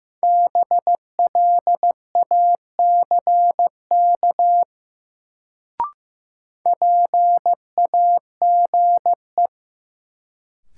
Morse code, heard as a keyed tone, about 15 wpm, 700 Hz. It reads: BLACK PAGE
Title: Sets of 5 Words Encore - New Word Drill - 15wpm